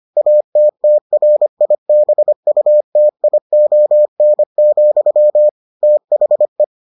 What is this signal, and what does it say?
Morse code, heard as a keyed tone, about 25 wpm, 600 Hz. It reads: ATTRIBUTION, THE